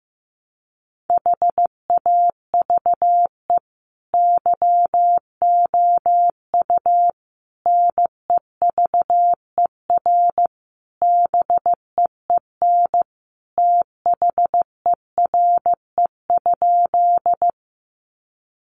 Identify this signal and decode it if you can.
Morse code, heard as a keyed tone, about 15 wpm, 700 Hz. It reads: HAVE YOU NEVER BEEN THERE?
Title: Sentences from Top 300 Words 15wpm